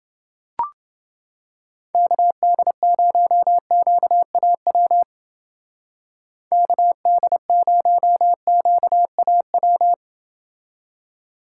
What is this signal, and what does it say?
Morse code, heard as a keyed tone, about 28 wpm, 700 Hz. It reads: KD0QAW KD0QAW